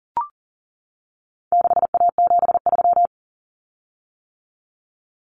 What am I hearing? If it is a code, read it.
Morse code, heard as a keyed tone, about 40 wpm, 700 Hz. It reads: 6A73
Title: Common Cut Numbers - 4-Digit 40wpm